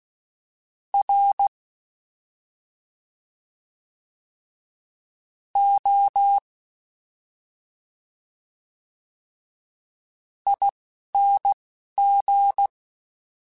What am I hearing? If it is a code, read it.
Morse code, heard as a keyed tone, about 16 wpm, spaced as for 4 wpm, 800 Hz. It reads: ROING
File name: Special INGs-mp3s / Special INGs-mp3s / Slow ing @10